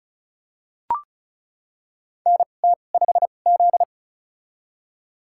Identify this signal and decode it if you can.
Morse code, heard as a keyed tone, about 36 wpm, spaced as for 25 wpm, 700 Hz. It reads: NT5Z
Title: USA Call Signs - 2x Character Spacing - 35wpm